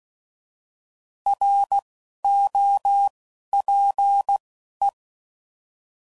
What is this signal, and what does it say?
Morse code, heard as a keyed tone, about 16 wpm, 800 Hz. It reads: R O P E
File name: Prefixes / By First Letter / Rs